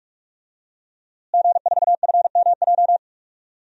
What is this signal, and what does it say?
Morse code, heard as a keyed tone, about 42 wpm, 700 Hz. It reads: G4FMJ
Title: DXCC-100 Call Signs - 5 Characters - Repeated 3x - 45wpm